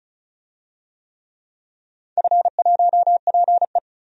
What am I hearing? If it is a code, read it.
Morse code, heard as a keyed tone, about 36 wpm, 690 Hz. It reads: F1PE
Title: DXCC-100 Call Signs - 4 Characters - 35wpm